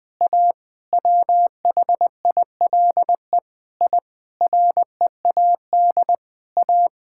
Morse code, heard as a keyed tone, about 20 wpm, 700 Hz. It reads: A WHILE I READ A
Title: Sentences with 5-Word Phrases - 20wpm